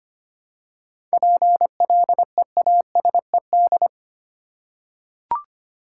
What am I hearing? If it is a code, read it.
Morse code, heard as a keyed tone, about 25 wpm, 700 Hz. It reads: PLEASED
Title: Seven Letter Words - 25wpm